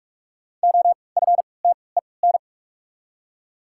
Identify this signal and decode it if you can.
Morse code, heard as a keyed tone, about 43 wpm, 700 Hz. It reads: O F T E N
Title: 3x Character Spacing - Top 300 Words - 45wpm